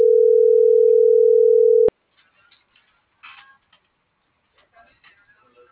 Ringback tone (North America), 440 + 480 Hz.